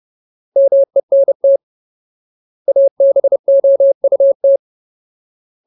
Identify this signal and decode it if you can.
Morse code, heard as a keyed tone, about 30 wpm, 550 Hz. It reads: MENT ABOUT
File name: Ultimate Word List / 30WPM Ultimate